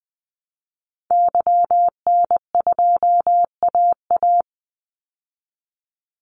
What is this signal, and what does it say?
Morse code, heard as a keyed tone, about 20 wpm, 700 Hz. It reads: YN2AA
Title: Call Signs Repeated 3x - 20wpm